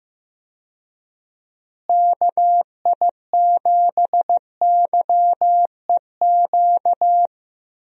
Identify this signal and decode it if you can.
Morse code, heard as a keyed tone, about 15 wpm, 700 Hz. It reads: KI7YEQ
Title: USA Call Signs - 6 Characters - Repeated 3x - 15wpm